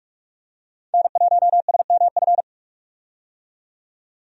Morse code, heard as a keyed tone, about 46 wpm, 700 Hz. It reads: N1SMF
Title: USA Call Signs - 5 Characters - 45wpm